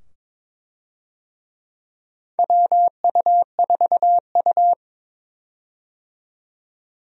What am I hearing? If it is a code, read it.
Morse code, heard as a keyed tone, about 22 wpm, 700 Hz. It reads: WU4U
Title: USA Call Signs - 4 Characters - 22wpm